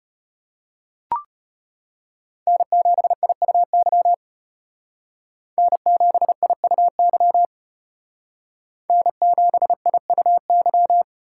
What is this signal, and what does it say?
Morse code, heard as a keyed tone, about 32 wpm, 700 Hz. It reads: N7IUY N7IUY N7IUY